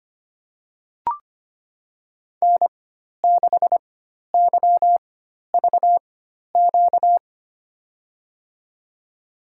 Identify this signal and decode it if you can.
Morse code, heard as a keyed tone, about 25 wpm, spaced as for 12 wpm, 700 Hz. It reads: N6YVQ
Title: USA Call Signs - 5 Characters - 4x Character Spacing - 25wpm